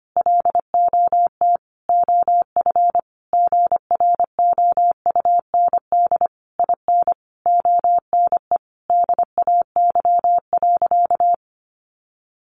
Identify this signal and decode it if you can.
Morse code, heard as a keyed tone, about 25 wpm, 700 Hz. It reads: LOT OF GROUND IN ONE DAY.
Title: Sentences from Top 500 Words 25wpm